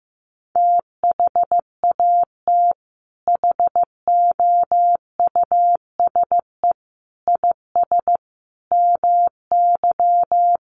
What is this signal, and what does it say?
Morse code, heard as a keyed tone, about 15 wpm, 700 Hz. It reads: THAT HOUSE IS MY
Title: Sentences from Top 1000 Words 15wpm